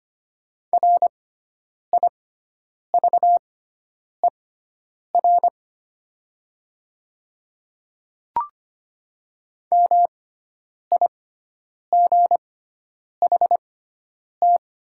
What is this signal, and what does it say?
Morse code, heard as a keyed tone, about 25 wpm, spaced as for 9 wpm, 700 Hz. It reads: RIVER MIGHT
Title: 6x Character Spacing - Top 300 Words - 25wpm